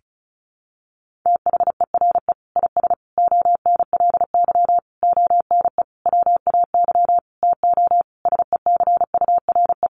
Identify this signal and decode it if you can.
Morse code, heard as a keyed tone, about 34 wpm, 700 Hz. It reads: THERE IS ONLY ONE WAY TO SECURE
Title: Sentences from Top 1200 Words 35wpm